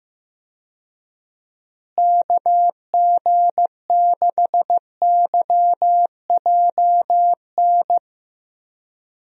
Morse code, heard as a keyed tone, about 15 wpm, 700 Hz. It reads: KG6YJN
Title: USA Call Signs - 6 Characters - Repeated 2x - 15wpm